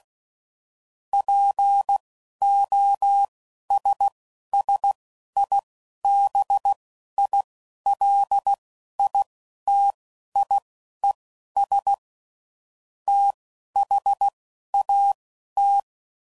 Morse code, heard as a keyed tone, about 16 wpm, spaced as for 11 wpm, 800 Hz. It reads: POSSIBILITIES THAT
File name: CW Code Groups @10WPM / Technical Subjects / OR Gate